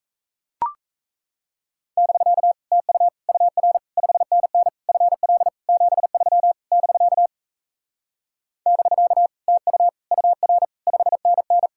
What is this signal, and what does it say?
Morse code, heard as a keyed tone, about 38 wpm, 700 Hz. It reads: <BK> TU UR 5NN FL 73 <BK> <BK> TU UR 5NN